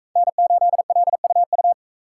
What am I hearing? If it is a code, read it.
Morse code, heard as a keyed tone, about 44 wpm, 700 Hz. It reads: N8LUU